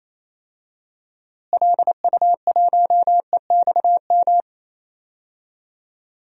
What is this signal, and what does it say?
Morse code, heard as a keyed tone, about 28 wpm, 700 Hz. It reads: LU1EXM
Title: DXCC-100 Call Signs - 6 Characters - Repeated 3x - 28wpm